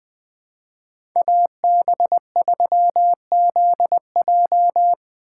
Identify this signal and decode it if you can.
Morse code, heard as a keyed tone, about 20 wpm, 700 Hz. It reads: AB3ZJ
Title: USA Call Signs - 5 Characters - 6x Character Spacing - 20wpm